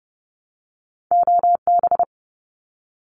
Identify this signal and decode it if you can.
Morse code, heard as a keyed tone, about 30 wpm, 700 Hz. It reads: OB
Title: Abbreviations Repeated 3x - 30wpm